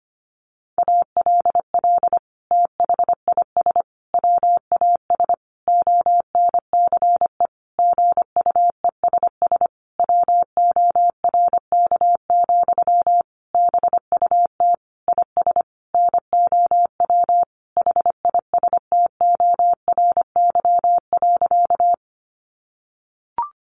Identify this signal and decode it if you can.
Morse code, heard as a keyed tone, about 25 wpm, 700 Hz. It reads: ALL THIS WAS ONCE GUESS WORK, BUT IS NOW HISTORY.
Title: Sentences from Top 1500 Words 25wpm